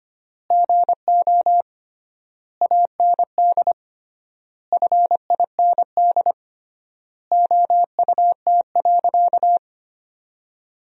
Morse code, heard as a keyed tone, about 25 wpm, 700 Hz. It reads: GO AND FIND OUT.
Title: 3x Word-Spacing - Sentences from Top 200 Words - 25wpm